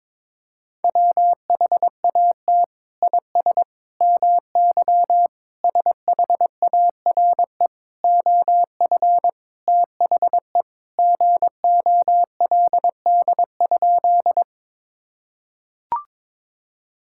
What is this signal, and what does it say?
Morse code, heard as a keyed tone, about 22 wpm, 700 Hz. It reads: WHAT IS MY SHARE OF THE GOLD?
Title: Sentences from Top 1000 Words - Review Part 6 - 22wpm